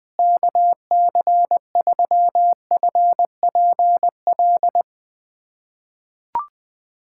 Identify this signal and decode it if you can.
Morse code, heard as a keyed tone, about 21 wpm, 700 Hz. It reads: KC3FPL
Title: USA Callsign DE Callsign - 6 Characters - 20wpm